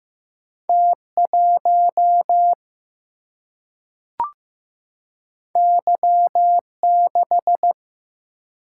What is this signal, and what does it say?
Morse code, heard as a keyed tone, about 15 wpm, 700 Hz. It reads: T1 Y6
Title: Single-Letter Single-Number 15wpm